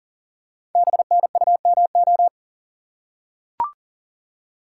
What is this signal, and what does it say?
Morse code, heard as a keyed tone, about 40 wpm, 700 Hz. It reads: BNUMO
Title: Code Groups - 5 Letters - 40wpm